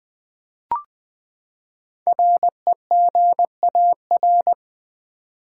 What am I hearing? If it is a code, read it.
Morse code, heard as a keyed tone, about 20 wpm, 700 Hz. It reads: REGAR
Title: Build A Word - 5 Letters At A Time - 10-Letter Words - 20wpm